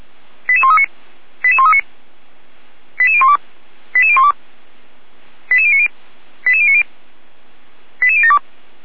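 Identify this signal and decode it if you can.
Five-tone selective call (ZVEI), heard as quick runs of five tones, tones of about 70 ms. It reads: 88139 88139 88012 88012 88090 88090 88072
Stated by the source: ZVEI five-tone selective call received by SDR, fire brigade Orlamünde (feuerwehr-orlamuende) server.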